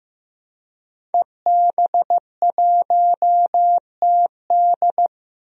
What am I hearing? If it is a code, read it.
Morse code, heard as a keyed tone, about 15 wpm, 700 Hz. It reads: EB1TD
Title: DXCC-100 Call Signs - 5 Characters - 15wpm